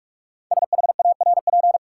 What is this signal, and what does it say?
Morse code, heard as a keyed tone, about 44 wpm, 700 Hz. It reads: SHARP
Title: Sentences from Top 1600 Words - New Word Drill - 45wpm